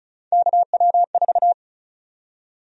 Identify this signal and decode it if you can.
Morse code, heard as a keyed tone, about 36 wpm, 700 Hz. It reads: KW4
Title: USA Call Sign Prefixes - 35wpm